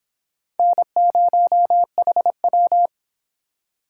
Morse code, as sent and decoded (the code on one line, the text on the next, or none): -. ----- .... .--
N0HW